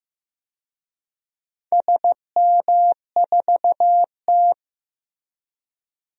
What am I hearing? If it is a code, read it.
Morse code, heard as a keyed tone, about 15 wpm, 700 Hz. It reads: SM4T
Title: DXCC-100 Call Signs - 4 Characters - 15wpm